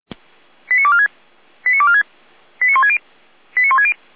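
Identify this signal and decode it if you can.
Five-tone selective call (ZVEI), heard as quick runs of five tones, tones of about 70 ms. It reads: 89246 89246 89160 89160
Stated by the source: ZVEI five-tone selective call received by SDR.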